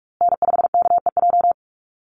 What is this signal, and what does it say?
Morse code, heard as a keyed tone, about 44 wpm, 700 Hz. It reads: N5KEJ
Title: USA Call Signs Repeated 2x 45wpm